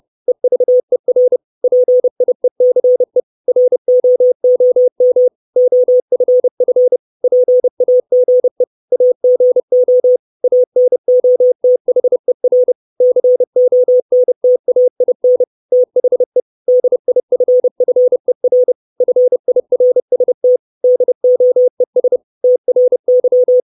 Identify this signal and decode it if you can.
Morse code, heard as a keyed tone, about 30 wpm, 500 Hz. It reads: EVER PIECE ROOM OFF PAGE AGO ANOTHER CONTAIN THE DIFFER FIRST DOES TRY
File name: 632 Words 30WPM